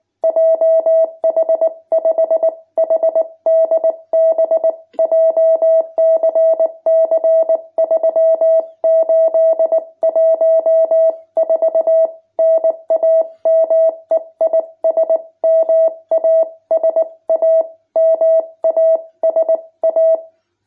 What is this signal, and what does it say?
Morse code, heard as a keyed tone, about 19 wpm, 650 Hz. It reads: JH5HDB JCC 3814 NAME IS MASA MASA
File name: jh8cbh / 2021cw